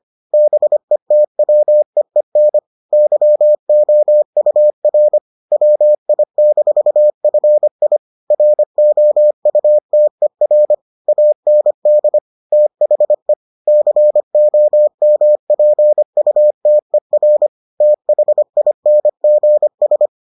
Morse code, heard as a keyed tone, about 25 wpm, 600 Hz. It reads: BETWEEN YOUR WI-FI ROUTER AND THE COMPUTER THINGS